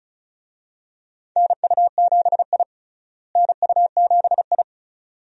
Morse code, heard as a keyed tone, about 36 wpm, 700 Hz. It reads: NU7I NU7I
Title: USA Call Signs - 4 Characters - Repeated 2x - 35wpm